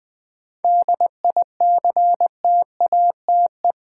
Morse code, heard as a keyed tone, about 20 wpm, 690 Hz. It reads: DICTATE
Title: 7 Letter Words - First 3 Letters Given - 20wpm